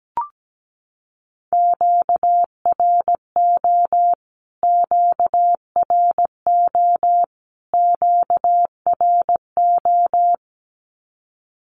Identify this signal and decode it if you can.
Morse code, heard as a keyed tone, about 17 wpm, 700 Hz. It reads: QRO QRO QRO
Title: Q-Codes Repeated 3x 17wpm